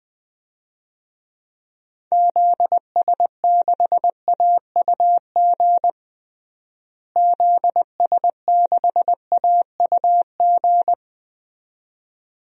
Morse code, heard as a keyed tone, about 20 wpm, 700 Hz. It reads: ZS6AUG ZS6AUG
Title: DXCC-100 Call Signs - 6 Characters - Repeated 2x - 20wpm